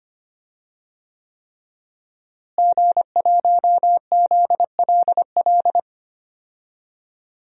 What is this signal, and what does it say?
Morse code, heard as a keyed tone, about 25 wpm, 700 Hz. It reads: G1ZLL
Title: DXCC-100 Call Signs - 5 Characters - Repeated 3x - 25wpm